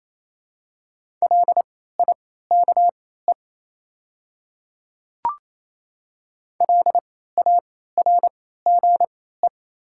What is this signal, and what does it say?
Morse code, heard as a keyed tone, about 28 wpm, spaced as for 16 wpm, 690 Hz. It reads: LIKE LARGE
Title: 3x Character Spacing - Top 200 Words - 28wpm